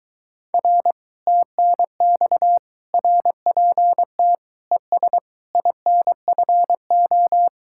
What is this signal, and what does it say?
Morse code, heard as a keyed tone, about 23 wpm, 700 Hz. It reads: R TNX RPT ES INFO